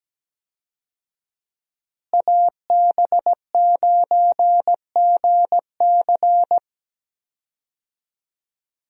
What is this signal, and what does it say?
Morse code, heard as a keyed tone, about 17 wpm, 700 Hz. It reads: AB9GC
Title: USA Call Signs - 5 Characters - Repeated 2x - 17wpm